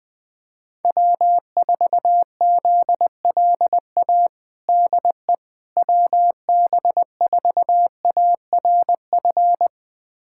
Morse code, heard as a keyed tone, about 20 wpm, 700 Hz. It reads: W4ZLA DE WB4ARF
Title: USA Callsign DE Callsign 20wpm